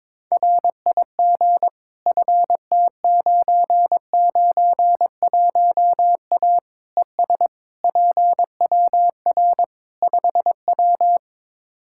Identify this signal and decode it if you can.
Morse code, heard as a keyed tone, about 22 wpm, 700 Hz. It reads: RIG FT991A ES PWR 5W
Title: Practice QSO - Exchange 2 - Info - 22wpm